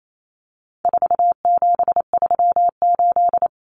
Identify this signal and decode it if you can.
Morse code, heard as a keyed tone, about 28 wpm, 700 Hz. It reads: 4738
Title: Numbers - 4-Digit 28wpm